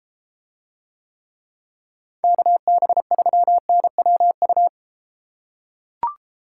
Morse code, heard as a keyed tone, about 32 wpm, 700 Hz. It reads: KB3NWU